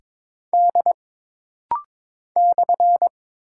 Morse code, heard as a keyed tone, about 22 wpm, 700 Hz. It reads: D /
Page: This rapid-fire timing is geared to push your ICR (Instant Character Recognition) skills.